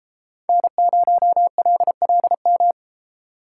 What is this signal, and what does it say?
Morse code, heard as a keyed tone, about 31 wpm, 700 Hz. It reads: N0LLM